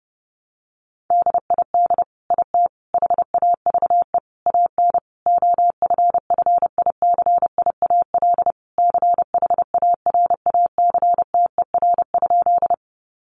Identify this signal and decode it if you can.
Morse code, heard as a keyed tone, about 30 wpm, 700 Hz. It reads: DID IT HAVE AN OFFICIAL CHARACTER?
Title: Sentences from Top 1200 Words 30wpm